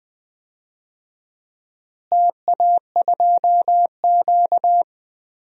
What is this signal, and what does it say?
Morse code, heard as a keyed tone, about 19 wpm, 700 Hz. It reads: TA2Q
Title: DXCC-100 Call Signs - 4 Characters - Repeated 2x - 20wpm